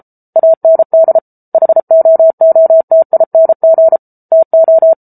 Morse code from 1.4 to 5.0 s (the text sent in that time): HOOTING TO